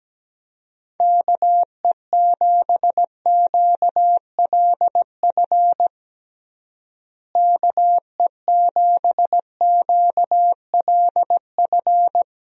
Morse code, heard as a keyed tone, about 17 wpm, 700 Hz. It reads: KE7QLF KE7QLF
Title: USA Call Signs - 6 Characters - Repeated 2x - 17wpm